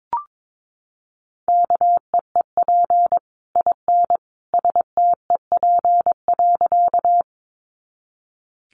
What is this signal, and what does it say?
Morse code, heard as a keyed tone, about 22 wpm, 700 Hz. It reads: KEEP IN STEP.
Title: Sentences from Top 400 Words 22wpm